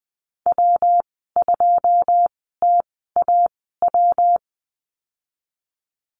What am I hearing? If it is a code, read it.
Morse code, heard as a keyed tone, about 20 wpm, spaced as for 15 wpm, 700 Hz. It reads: W2TAW
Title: 128 Lesson 21g - Number 6 - Call Signs 20wpm